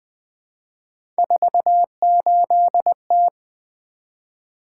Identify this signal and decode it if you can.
Morse code, heard as a keyed tone, about 20 wpm, 700 Hz. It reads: 48T
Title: Three Letter-Number Combinations - 20wpm